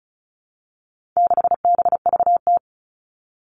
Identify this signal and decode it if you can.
Morse code, heard as a keyed tone, about 34 wpm, 700 Hz. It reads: 6BVT
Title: All Cut Numbers - 4-Digit 35wpm